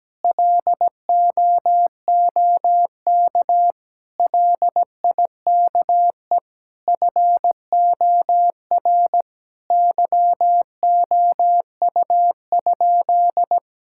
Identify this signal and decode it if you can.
Morse code, heard as a keyed tone, about 17 wpm, 700 Hz. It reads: LOOK LIKE FOR YOU?
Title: Sentences from Top 1000 Words - Review Part 6 - 17wpm